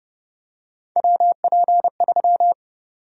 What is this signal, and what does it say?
Morse code, heard as a keyed tone, about 30 wpm, 700 Hz. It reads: WP3
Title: USA Call Sign Prefixes - Repeated 2x - 30wpm